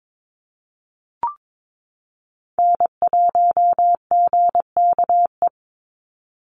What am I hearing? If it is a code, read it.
Morse code, heard as a keyed tone, about 22 wpm, 700 Hz. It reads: N1GKE
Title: Call Signs Repeated 3x - 22wpm